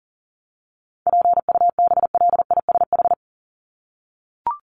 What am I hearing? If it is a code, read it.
Morse code, heard as a keyed tone, about 40 wpm, 700 Hz. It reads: PUBLISH
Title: ED Suffix - Top 100 Words with Base Words 40wpm